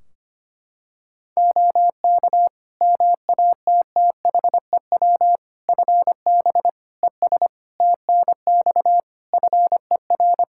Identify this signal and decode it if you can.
Morse code, heard as a keyed tone, about 25 wpm, 700 Hz. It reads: OK MATTHEW FB ES TNX FER
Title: Practice QSO - Exchange 2 - Info - 25wpm